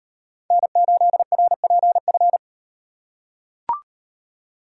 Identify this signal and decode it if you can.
Morse code, heard as a keyed tone, about 38 wpm, 700 Hz. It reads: N8RPF